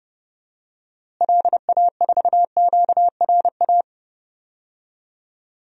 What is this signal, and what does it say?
Morse code, heard as a keyed tone, about 30 wpm, 700 Hz. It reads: LA4QRA
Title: DXCC-100 Call Signs - 6 Characters - Repeated 3x - 30wpm